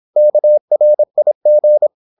Morse code, heard as a keyed tone, about 26 wpm, 600 Hz. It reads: KRIG
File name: Korte ord DK 26wpm